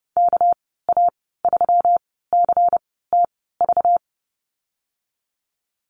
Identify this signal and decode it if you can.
Morse code, heard as a keyed tone, about 30 wpm, spaced as for 17 wpm, 700 Hz. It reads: KA3CTV
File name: Lesson 37g - Letter x - Call Signs 30wpm